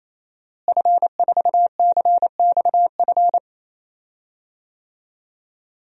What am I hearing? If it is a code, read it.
Morse code, heard as a keyed tone, about 28 wpm, 700 Hz. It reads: F4CXF
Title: DXCC-100 Call Signs - 5 Characters - Repeated 3x - 28wpm